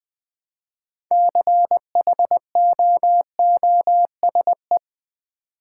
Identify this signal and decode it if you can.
Morse code, heard as a keyed tone, about 20 wpm, 700 Hz. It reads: CHOOSE
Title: Six Letter Words - 20wpm